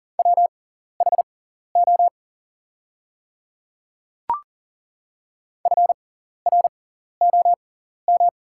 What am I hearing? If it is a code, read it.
Morse code, heard as a keyed tone, about 37 wpm, spaced as for 14 wpm, 700 Hz. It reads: WHO FROM
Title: 6x Character Spacing - Top 100 Words - 40wpm